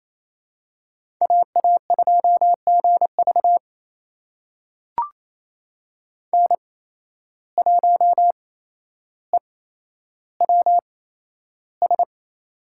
Morse code, heard as a keyed tone, about 28 wpm, spaced as for 12 wpm, 700 Hz. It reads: AA2GV N1EWS
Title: USA Call Signs - 5 Characters - 8x Character Spacing - 28wpm